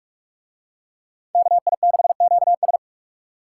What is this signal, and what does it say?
Morse code, heard as a keyed tone, about 41 wpm, 700 Hz. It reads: KI6QS